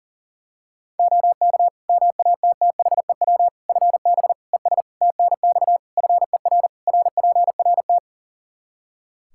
Morse code, heard as a keyed tone, about 40 wpm, 700 Hz. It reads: OK MATTHEW FB ES TNX FER RPRT
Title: Practice QSO - Exchange 2 - Info - 40wpm